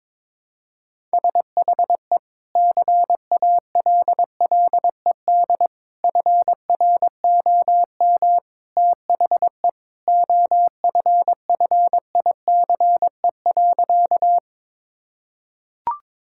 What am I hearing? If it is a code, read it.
Morse code, heard as a keyed tone, about 22 wpm, 700 Hz. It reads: SHE CALLED FROM THE OFFICE.
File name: practice / 22wpm / Sentences with 2-Word Phrases - 22wpm